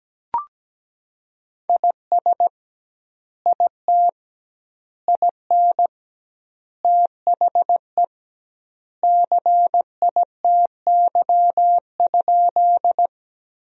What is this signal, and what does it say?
Morse code, heard as a keyed tone, about 17 wpm, 700 Hz. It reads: IS IT IN THE CITY?
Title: 2x Word-Spacing - Sentences from Top 300 Words - 17wpm